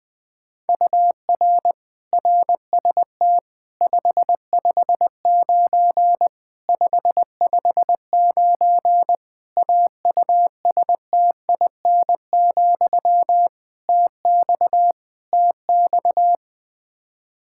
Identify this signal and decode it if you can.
Morse code, heard as a keyed tone, about 20 wpm, 700 Hz. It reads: UR RST 559 559 AUSTIN, TX TX
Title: K1USN SST QSO Practice 20wpm